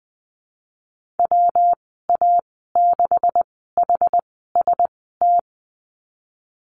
Morse code, heard as a keyed tone, about 20 wpm, spaced as for 15 wpm, 700 Hz. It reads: WA6HST
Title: 147 Lesson 24g - Letter y - Call Signs 20wpm